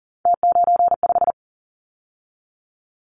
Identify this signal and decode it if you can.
Morse code, heard as a keyed tone, about 42 wpm, 700 Hz. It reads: T95
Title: Callsign Prefixes for Top 100 DXCC - 40wpm